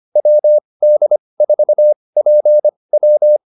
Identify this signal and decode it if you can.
Morse code, heard as a keyed tone, about 25 wpm, spaced as for 20 wpm, 600 Hz. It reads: WD4PW
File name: QSO303a 20